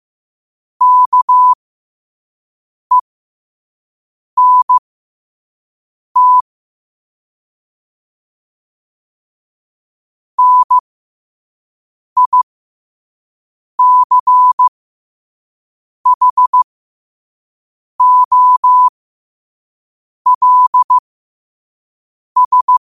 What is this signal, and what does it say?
Morse code, heard as a keyed tone, about 15 wpm, spaced as for 5 wpm, 1000 Hz. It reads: KENT NICHOLS
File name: qotdfiles / 5wpm